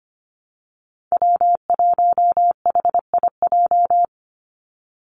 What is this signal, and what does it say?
Morse code, heard as a keyed tone, about 25 wpm, 700 Hz. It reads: W1HIJ
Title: Call Signs Repeated 3x - 25wpm